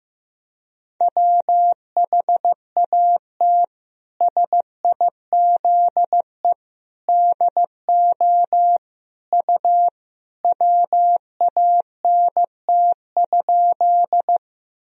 Morse code, heard as a keyed tone, about 15 wpm, 700 Hz. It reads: WHAT SIZE DO U WANT?
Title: Sentences from Top 400 Words and Abbreviations 15wpm